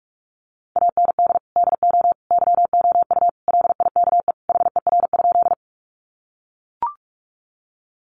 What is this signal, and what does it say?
Morse code, heard as a keyed tone, about 44 wpm, 700 Hz. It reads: AND DO YOU LIKE HER?
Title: Sentences from Top 100 Words 45wpm